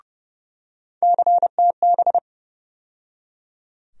Morse code, heard as a keyed tone, about 30 wpm, 700 Hz. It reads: CTB
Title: USA Call Sign Postfixes - 3 Letters 30wpm